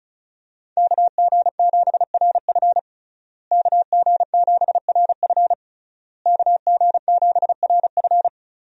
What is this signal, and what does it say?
Morse code, heard as a keyed tone, about 34 wpm, 700 Hz. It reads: KG7RF KG7RF KG7RF
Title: USA Call Signs - 5 Characters - Repeated 3x - 35wpm